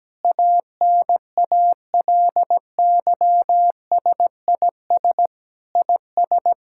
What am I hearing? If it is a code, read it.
Morse code, heard as a keyed tone, about 17 wpm, 700 Hz. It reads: ANALYSIS IS
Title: Sentences from Top 1000 Words - Review Part 5 - 17wpm